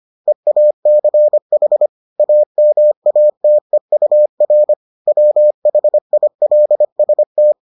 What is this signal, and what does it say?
Morse code, heard as a keyed tone, about 25 wpm, 600 Hz. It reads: EACH AMATEUR WHILST